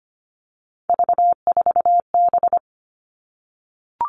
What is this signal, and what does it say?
Morse code, heard as a keyed tone, about 25 wpm, 700 Hz. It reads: V4B
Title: All Cut Numbers - 3-Digit 25wpm